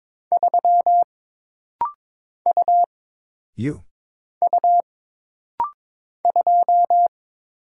Morse code, heard as a keyed tone, about 22 wpm, 700 Hz. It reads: 3 U U 2
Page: This rapid-fire timing is geared to push your ICR (Instant Character Recognition) skills.